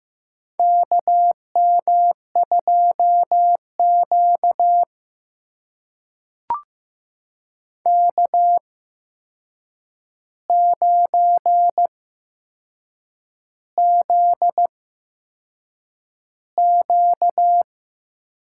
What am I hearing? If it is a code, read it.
Morse code, heard as a keyed tone, about 15 wpm, spaced as for 6 wpm, 700 Hz. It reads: KM2Q K9ZQ
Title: USA Call Signs - 4 Characters - 8x Character Spacing - 15wpm